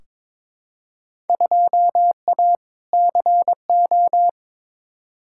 Morse code, heard as a keyed tone, about 22 wpm, 700 Hz. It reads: 2A CO
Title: Field Day Exchanges - 22wpm